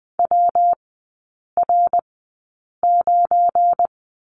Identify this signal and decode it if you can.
Morse code, heard as a keyed tone, about 20 wpm, spaced as for 8 wpm, 700 Hz. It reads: WR9
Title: 178 Lesson 29e - Slash - Three Characters 20wpm